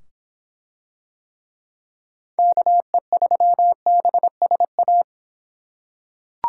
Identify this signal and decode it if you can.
Morse code, heard as a keyed tone, about 24 wpm, 700 Hz. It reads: KE3BSA